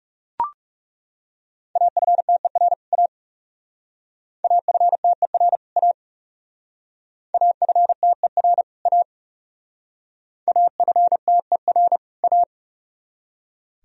Morse code, heard as a keyed tone, about 39 wpm, 700 Hz. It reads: AFTER A AFTER A AFTER A AFTER A